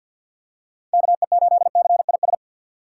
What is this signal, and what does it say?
Morse code, heard as a keyed tone, about 47 wpm, 700 Hz. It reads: KE8CIS
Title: USA Call Signs - 6 Characters - 4x Character Spacing - 50wpm